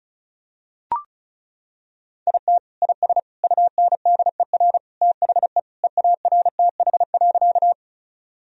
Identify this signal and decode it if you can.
Morse code, heard as a keyed tone, about 34 wpm, 700 Hz. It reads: IT IS UNDER THE EARTH.
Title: Sentences from Top 1000 Words - Review Part 1 - 35wpm